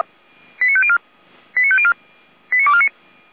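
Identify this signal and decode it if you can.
Five-tone selective call (ZVEI), heard as quick runs of five tones, tones of about 70 ms. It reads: 89504 89504 89249
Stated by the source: ZVEI five-tone selective call received by SDR, hosted by feuerwehr-orlamuende (Orlamünde fire brigade).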